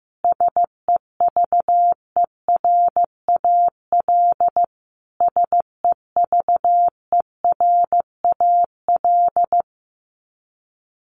Morse code, heard as a keyed tone, about 15 wpm, 700 Hz. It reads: SEVERAL SEVERAL
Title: Top 500 Words Repeated 15wpm